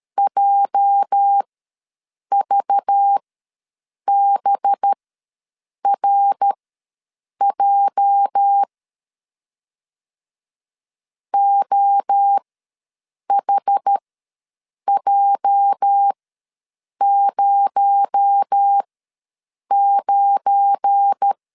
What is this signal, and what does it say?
Morse code, heard as a keyed tone, about 13 wpm, spaced as for 7 wpm, 800 Hz. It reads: JVBRJ OHJ09
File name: LEZIONI / 8 YJ45 / 8 YJ ws 12wpm 3342